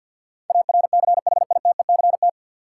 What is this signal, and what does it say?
Morse code, heard as a keyed tone, about 51 wpm, 700 Hz. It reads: ARCHITECT
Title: K1USN SST QSO Practice 50wpm